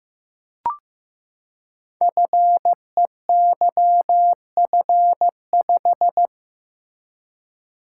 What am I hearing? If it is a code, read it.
Morse code, heard as a keyed tone, about 15 wpm, 700 Hz. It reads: FEYF5
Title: Code Groups - 5 Letter:Numbers - 15wpm